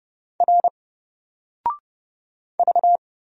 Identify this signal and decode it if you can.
Morse code, heard as a keyed tone, about 30 wpm, 700 Hz. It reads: R V
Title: Single Character (Letter:Number:Punctuation:Prosign) 30wpm